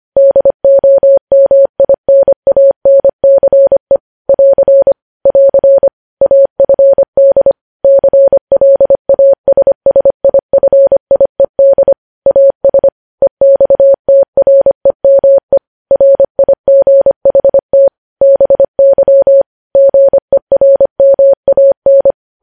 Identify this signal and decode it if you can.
Morse code, heard as a keyed tone, about 25 wpm, 550 Hz. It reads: DOMINANCE <AR> <AR> AFD CLASSIFIED AS EXTREME RIGHT BY GERMAN
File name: podcast / News headlines-1x-25x25